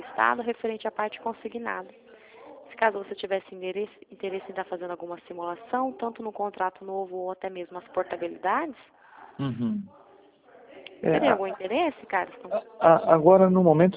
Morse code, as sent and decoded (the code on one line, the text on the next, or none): none